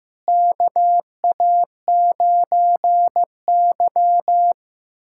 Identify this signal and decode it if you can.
Morse code, heard as a keyed tone, about 15 wpm, 700 Hz. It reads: KA9Y
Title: USA Callsign DE Callsign - 4 Characters - 15wpm